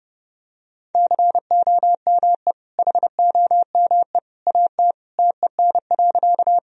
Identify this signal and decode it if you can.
Morse code, heard as a keyed tone, about 30 wpm, 700 Hz. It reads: COME HOME AT TEN.
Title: Sentences from Top 1000 Words - Review Part 2 - 30wpm